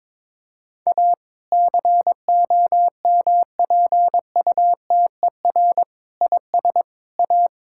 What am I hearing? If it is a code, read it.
Morse code, heard as a keyed tone, about 22 wpm, 700 Hz. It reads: A COMPUTER IS A
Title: Sentences from Top 1000 Words - Review Part 6 - 22wpm